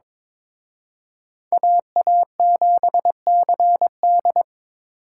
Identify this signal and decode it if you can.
Morse code, heard as a keyed tone, about 26 wpm, 700 Hz. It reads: AA7CD
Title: Speed Racing - USA 2x2 Call Signs 22wpm